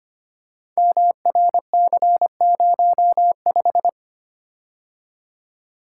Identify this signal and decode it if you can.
Morse code, heard as a keyed tone, about 25 wpm, 700 Hz. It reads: MRC05
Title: Code Groups - 5 Letter:Numbers - 25wpm